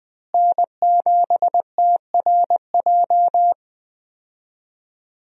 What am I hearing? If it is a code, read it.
Morse code, heard as a keyed tone, about 20 wpm, 700 Hz. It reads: N7TRJ
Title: USA Call Signs - 5 Characters - Repeated 3x - 20wpm